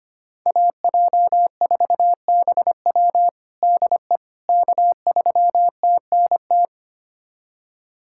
Morse code, heard as a keyed tone, about 25 wpm, 700 Hz. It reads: AJ4BW DE K3TNT
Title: USA Callsign DE Callsign - 5 Characters - 25wpm